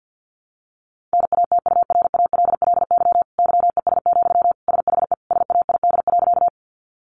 Morse code, heard as a keyed tone, about 50 wpm, 700 Hz. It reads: NATURALLY YES, SHE SAID.
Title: Sentences from Top 1300 Words 50wpm